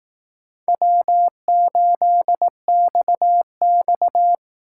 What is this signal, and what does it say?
Morse code, heard as a keyed tone, about 18 wpm, 700 Hz. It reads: W8XX
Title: Speed Racing - USA Call Signs 15wpm